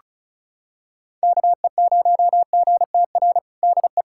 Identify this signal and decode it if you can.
Morse code, heard as a keyed tone, about 36 wpm, 700 Hz. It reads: KE0GTR DE
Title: USA Callsign DE Callsign - 6 Characters - 35wpm